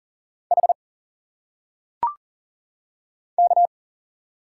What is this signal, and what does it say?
Morse code, heard as a keyed tone, about 40 wpm, 700 Hz. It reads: H K
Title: Single Letters - Phonetic Pronunciation 40wpm